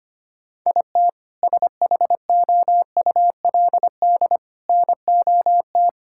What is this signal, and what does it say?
Morse code, heard as a keyed tone, about 25 wpm, 700 Hz. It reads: IT SHOULD NOT
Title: Sentences from Top 1600 Words 25wpm